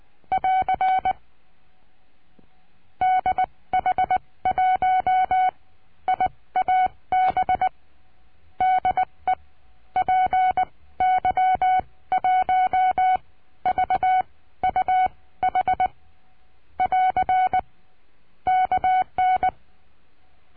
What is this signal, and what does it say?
Morse code, heard as a keyed tone, about 20 wpm, spaced as for 15 wpm, 740 Hz. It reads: <AR> DH1IAB DE PY1VUH <AR> KN
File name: cursos / CURSO DE TELEGRAFIA / Telegrafia - Trilha 22